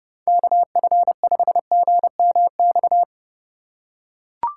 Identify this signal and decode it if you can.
Morse code, heard as a keyed tone, about 32 wpm, 700 Hz. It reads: KF5GMX
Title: USA Call Signs - 4x Character Spacing - 30wpm